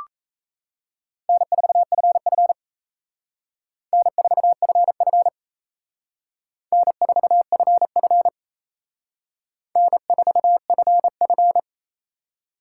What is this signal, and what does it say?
Morse code, heard as a keyed tone, about 35 wpm, 700 Hz. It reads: N4FF N4FF N4FF N4FF
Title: Speed Racing - USA 1x2 Call Signs 28wpm